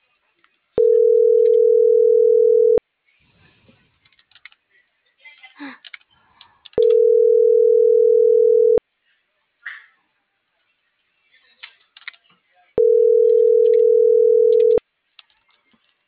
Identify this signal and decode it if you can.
Ringback tone (North America), 440 + 480 Hz.